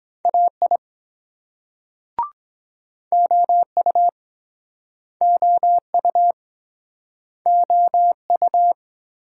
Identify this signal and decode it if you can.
Morse code, heard as a keyed tone, about 22 wpm, 700 Hz. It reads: AI OU OU OU